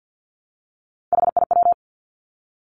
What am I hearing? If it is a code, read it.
Morse code, heard as a keyed tone, about 50 wpm, 700 Hz. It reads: HIW
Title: Three Challenging Letter-Number Combinations 50wpm